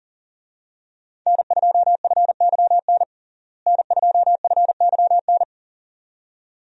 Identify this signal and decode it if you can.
Morse code, heard as a keyed tone, about 40 wpm, 700 Hz. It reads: N2FYN N2FYN
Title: USA Call Signs - 5 Characters - Repeated 2x - 40wpm